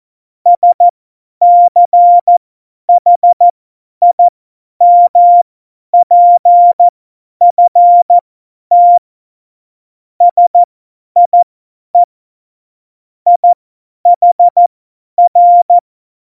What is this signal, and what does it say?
Morse code, heard as a keyed tone, about 14 wpm, spaced as for 10 wpm, 700 Hz. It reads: SCHIMPFT SIE IHR
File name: Morsekurs / Witze / Witz 18. September 2018, 50 F70